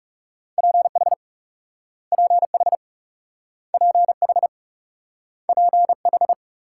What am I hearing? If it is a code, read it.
Morse code, heard as a keyed tone, about 41 wpm, 700 Hz. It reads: PH PH PH PH